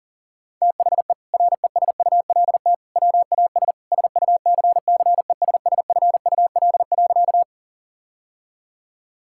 Morse code, heard as a keyed tone, about 37 wpm, 700 Hz. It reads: THE RESULT WAS SUCCESSFUL.
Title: Sentences from Top 1000 Words - Review Part 6 - 40wpm